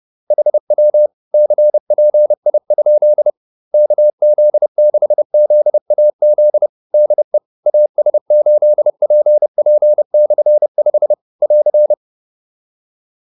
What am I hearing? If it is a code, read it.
Morse code, heard as a keyed tone, about 30 wpm, 600 Hz. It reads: HW CPI? KZ6ZAZ DE AS8PP/5 <AR>
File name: QSO302 30